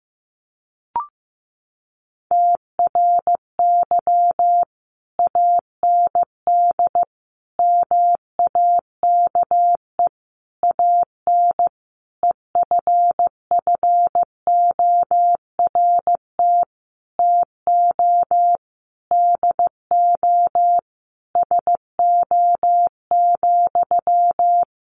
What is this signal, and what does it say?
Morse code, heard as a keyed tone, about 15 wpm, 700 Hz. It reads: TRY AND MAKE AN EFFORT TO DO SO,
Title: Sentences from Top 1200 Words 15wpm